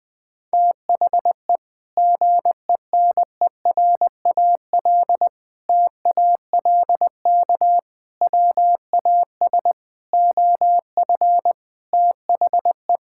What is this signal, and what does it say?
Morse code, heard as a keyed tone, about 20 wpm, 700 Hz. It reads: THE GENERAL TALK WAS OF THE
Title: Sentences from Top 1000 Words - Review Part 5 - 20wpm